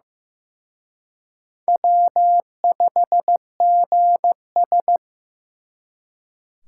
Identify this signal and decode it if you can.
Morse code, heard as a keyed tone, about 18 wpm, 700 Hz. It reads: W5GS
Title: Speed Racing - USA Call Signs 15wpm